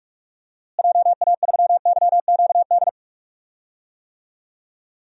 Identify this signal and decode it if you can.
Morse code, heard as a keyed tone, about 41 wpm, 700 Hz. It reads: JA3YQD
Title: DXCC-100 Call Signs - 6 Characters - Repeated 3x - 45wpm